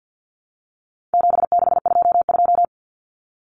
Morse code, heard as a keyed tone, about 50 wpm, 700 Hz. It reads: 7622
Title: Common Cut Numbers - 4-Digit 50wpm